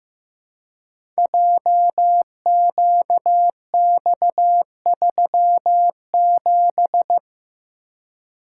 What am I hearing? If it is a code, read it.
Morse code, heard as a keyed tone, about 15 wpm, 700 Hz. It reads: JQX37
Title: Code Groups - 5 Letter:Numbers - 15wpm